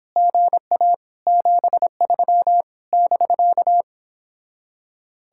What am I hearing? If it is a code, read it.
Morse code, heard as a keyed tone, about 31 wpm, 700 Hz. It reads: GA 73 <BK>